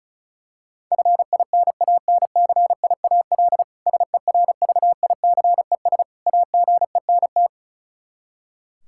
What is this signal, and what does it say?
Morse code, heard as a keyed tone, about 34 wpm, 700 Hz. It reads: FINANCIAL SERVICES AGENT
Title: Practice QSO - Exchange 3 - More Info - 35wpm